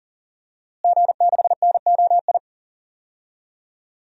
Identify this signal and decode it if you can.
Morse code, heard as a keyed tone, about 40 wpm, 700 Hz. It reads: G6NOI
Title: DXCC-100 Call Signs - 5 Characters - Repeated 3x - 40wpm